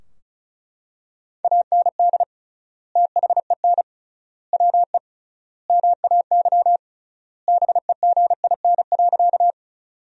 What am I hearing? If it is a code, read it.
Morse code, heard as a keyed tone, about 36 wpm, 700 Hz. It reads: AND THEN WE MAY BEGIN.
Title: 3x Word-Spacing - Sentences from Top 300 Words - 35wpm